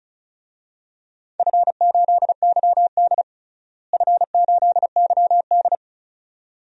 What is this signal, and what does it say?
Morse code, heard as a keyed tone, about 34 wpm, 700 Hz. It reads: F8YD F8YD
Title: DXCC-100 Call Signs - 4 Characters - Repeated 2x - 35wpm